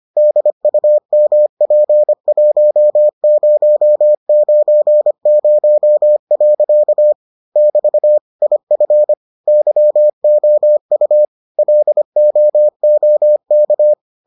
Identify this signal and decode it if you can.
Morse code, heard as a keyed tone, about 25 wpm, 600 Hz. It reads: DUMP1090. <BT> IF YOU LOOK